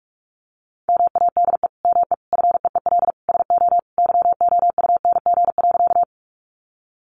Morse code, heard as a keyed tone, about 44 wpm, 700 Hz. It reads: MADE ME FEEL SO YOUNG.
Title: Sentences from Top 400 Words 45wpm